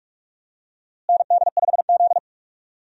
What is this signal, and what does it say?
Morse code, heard as a keyed tone, about 45 wpm, 700 Hz. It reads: ND5Z
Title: USA Call Signs - 4 Characters - Repeated 2x - 45wpm